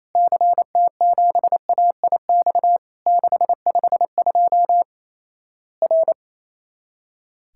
Morse code, heard as a keyed tone, about 28 wpm, 700 Hz. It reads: CT7AIX 652 R
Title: Worked All Europe DX Contest - QTC Exchange - 28wpm